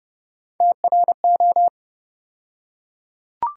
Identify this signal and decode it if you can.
Morse code, heard as a keyed tone, about 30 wpm, 700 Hz. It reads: TRO
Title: USA Call Sign Postfixes - 3 Letters 30wpm